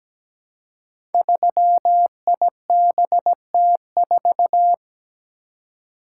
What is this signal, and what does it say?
Morse code, heard as a keyed tone, about 17 wpm, 700 Hz. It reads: 3IBT4
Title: Code Groups - 5 Letter:Numbers - 17wpm